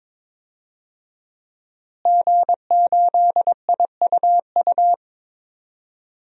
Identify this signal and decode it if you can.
Morse code, heard as a keyed tone, about 22 wpm, 700 Hz. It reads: G8IUU
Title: DXCC-100 Call Signs - 5 Characters - Repeated 2x - 22wpm